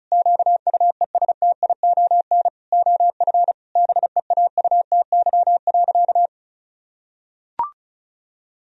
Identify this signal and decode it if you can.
Morse code, heard as a keyed tone, about 34 wpm, 700 Hz. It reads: QUESTION OF BEAUTY.
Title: Sentences from Top 1000 Words - Review Part 2 - 35wpm